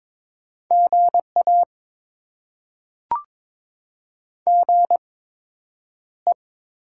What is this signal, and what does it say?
Morse code, heard as a keyed tone, about 22 wpm, spaced as for 9 wpm, 700 Hz. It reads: GA GE
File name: Top 20 QSO Elements - 8x Character Spacing - 22wpm